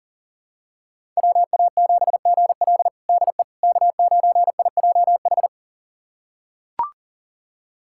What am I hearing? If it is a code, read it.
Morse code, heard as a keyed tone, about 40 wpm, 700 Hz. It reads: WA7GL DE K9IJH
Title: USA Callsign DE Callsign - 5 Characters - 40wpm